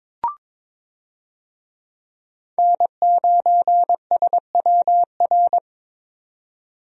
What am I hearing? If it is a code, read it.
Morse code, heard as a keyed tone, about 22 wpm, 700 Hz. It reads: N9SWR
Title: USA Call Signs - 5 Characters - 22wpm